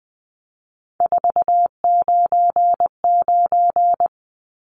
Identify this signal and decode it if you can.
Morse code, heard as a keyed tone, about 20 wpm, 700 Hz. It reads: 499
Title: Numbers - 3-Digit 20wpm